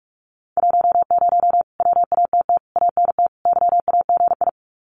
Morse code, heard as a keyed tone, about 44 wpm, 700 Hz. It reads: 10 WATT ANT YAGI
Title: Common QSO Phrases Encore 45wpm